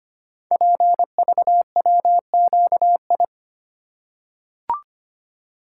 Morse code, heard as a keyed tone, about 25 wpm, 700 Hz. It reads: PVWQI
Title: Code Groups - 5 Letters - 25wpm